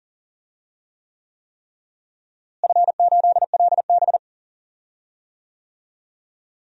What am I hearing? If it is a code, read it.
Morse code, heard as a keyed tone, about 40 wpm, 700 Hz. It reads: F8LB
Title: DXCC-100 Call Signs - 4 Characters - Repeated 3x - 40wpm